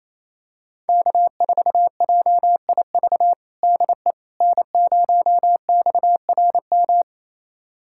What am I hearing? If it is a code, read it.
Morse code, heard as a keyed tone, about 28 wpm, 700 Hz. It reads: K4JIV DE N0XRM
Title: USA Callsign DE Callsign - 5 Characters - 28wpm